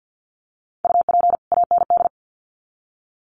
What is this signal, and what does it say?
Morse code, heard as a keyed tone, about 50 wpm, 700 Hz. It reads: UP AND
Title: Sets of 2 Words 50wpm